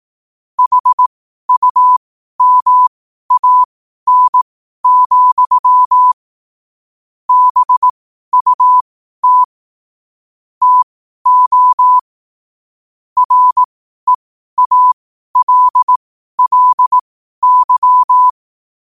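Morse code, heard as a keyed tone, about 18 wpm, spaced as for 12 wpm, 1000 Hz. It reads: HUMAN, BUT TO REALLY
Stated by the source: Morse code quotes at 12 words per minute.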